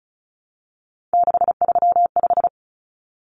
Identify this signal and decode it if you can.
Morse code, heard as a keyed tone, about 34 wpm, 700 Hz. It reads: 635
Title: Common Cut Numbers - 3-Digit 35wpm